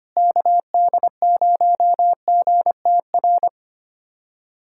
Morse code, heard as a keyed tone, about 25 wpm, 700 Hz. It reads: KD0GTR
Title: USA Callsign DE Callsign - 6 Characters - 25wpm